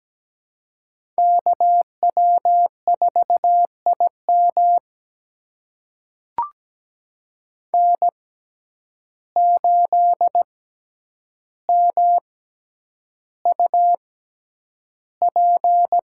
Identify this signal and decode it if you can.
Morse code, heard as a keyed tone, about 17 wpm, spaced as for 9 wpm, 700 Hz. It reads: KW4IM N8MUP
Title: USA Call Signs - 5 Characters - 6x Character Spacing - 17wpm